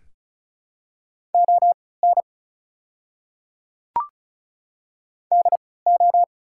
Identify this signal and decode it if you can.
Morse code, heard as a keyed tone, about 36 wpm, spaced as for 20 wpm, 700 Hz. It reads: ON DO
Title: 3x Character Spacing - Top 200 Words - 35wpm